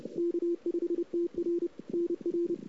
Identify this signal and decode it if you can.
Morse code, heard as a keyed tone, about 31 wpm, 380 Hz. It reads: K5TR NR
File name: CW / 2008-ss-cw-k5tr-op-n5rz -00:00